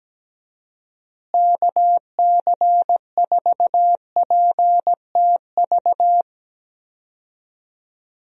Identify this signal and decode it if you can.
Morse code, heard as a keyed tone, about 17 wpm, 700 Hz. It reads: KC4PTV
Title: USA Call Signs - 6 Characters - Repeated 2x - 17wpm